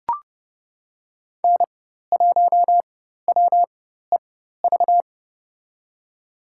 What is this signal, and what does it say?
Morse code, heard as a keyed tone, about 29 wpm, spaced as for 14 wpm, 700 Hz. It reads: N1WEV
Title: USA Call Signs - 5 Characters - 4x Character Spacing - 30wpm